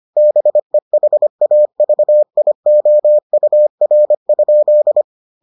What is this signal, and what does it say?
Morse code, heard as a keyed tone, about 25 wpm, 600 Hz. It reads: BEHAVIOUR?